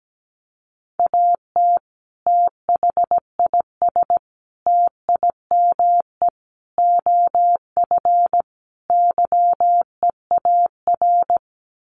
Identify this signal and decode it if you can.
Morse code, heard as a keyed tone, about 17 wpm, 700 Hz. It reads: AT THIS TIME OF YEAR
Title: Sets of 5 Words Encore2 17wpm